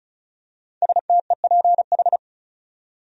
Morse code, heard as a keyed tone, about 34 wpm, 700 Hz. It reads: STEPH
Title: USA Common Names - 35wpm